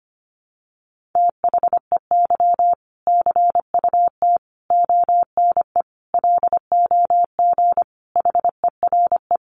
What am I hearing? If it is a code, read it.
Morse code, heard as a keyed tone, about 25 wpm, 700 Hz. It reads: THEY CUT ONE LOG HERE
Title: Sentences from Top 1400 Words 25wpm